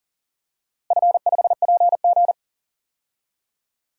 Morse code, heard as a keyed tone, about 40 wpm, 700 Hz. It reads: F5PG
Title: DXCC-100 Call Signs - 4 Characters - Repeated 3x - 40wpm